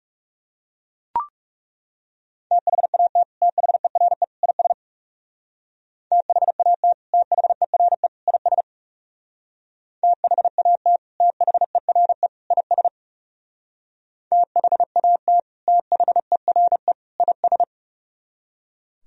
Morse code, heard as a keyed tone, about 40 wpm, 700 Hz. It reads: THAT THERE IS THAT THERE IS THAT THERE IS THAT THERE IS